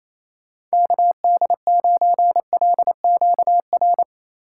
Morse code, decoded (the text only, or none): KD9LQR